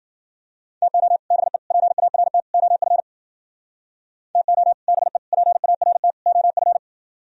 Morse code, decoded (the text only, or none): TO BE PART OF TO BE PART OF